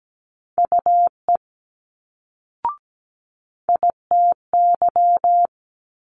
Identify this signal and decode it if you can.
Morse code, heard as a keyed tone, about 17 wpm, 700 Hz. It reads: UE ITY
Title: Multi-letter Phonemes Quick Fire 17wpm